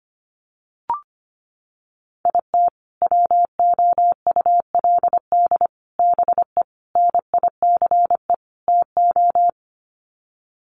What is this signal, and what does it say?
Morse code, heard as a keyed tone, about 25 wpm, 700 Hz. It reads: IT WOULD BE NICE TO
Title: Sets of 5 Words Encore2 25wpm